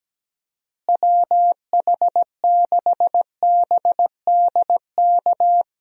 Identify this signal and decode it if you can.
Morse code, heard as a keyed tone, about 17 wpm, 700 Hz. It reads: WH6BDK
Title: USA Call Signs - 6 Characters - Repeated 2x - 17wpm